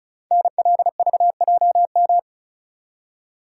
Morse code, heard as a keyed tone, about 36 wpm, 700 Hz. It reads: NLVJM
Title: Code Groups - 5 Letters - 35wpm